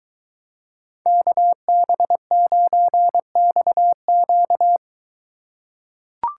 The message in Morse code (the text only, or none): KB9XQ